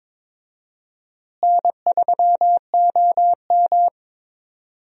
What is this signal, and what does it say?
Morse code, heard as a keyed tone, about 22 wpm, 700 Hz. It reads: N3OM
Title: USA Call Signs - 4 Characters - Repeated 3x - 22wpm